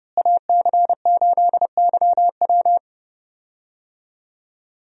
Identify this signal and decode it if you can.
Morse code, heard as a keyed tone, about 30 wpm, 700 Hz. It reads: AC8YW
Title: USA Call Signs - 5 Characters - Repeated 2x - 30wpm